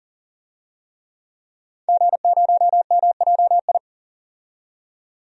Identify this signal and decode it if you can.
Morse code, heard as a keyed tone, about 37 wpm, 700 Hz. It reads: G0MJI
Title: DXCC-100 Call Signs - 5 Characters - 40wpm